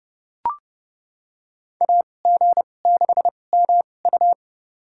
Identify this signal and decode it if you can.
Morse code, heard as a keyed tone, about 30 wpm, spaced as for 22 wpm, 700 Hz. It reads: AG6MU
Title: USA Call Signs - 5 Characters - 2x Character Spacing - 30wpm